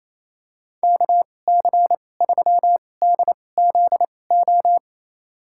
Morse code, heard as a keyed tone, about 28 wpm, spaced as for 20 wpm, 700 Hz. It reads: KC3DZO
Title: USA Call Signs - 2x Character Spacing - 28wpm